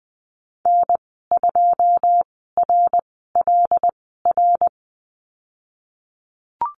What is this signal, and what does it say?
Morse code, heard as a keyed tone, about 20 wpm, spaced as for 15 wpm, 700 Hz. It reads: N2RLR
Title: 128 Lesson 21g - Number 6 - Call Signs 20wpm